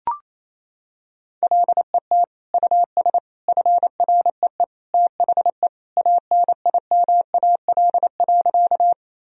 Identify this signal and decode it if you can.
Morse code, heard as a keyed tone, about 28 wpm, 700 Hz. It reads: LET US FREE THE ANIMAL.
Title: Sentences from Top 1000 Words - Review Part 1 - 28wpm